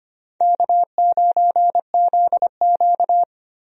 Morse code, heard as a keyed tone, about 25 wpm, 700 Hz. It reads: K9ZQ
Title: USA Call Signs - 4 Characters - 25wpm